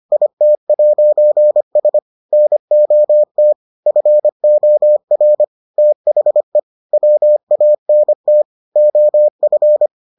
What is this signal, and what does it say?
Morse code, heard as a keyed tone, about 25 wpm, 600 Hz. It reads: IT'S NOT FOR THE WANT OF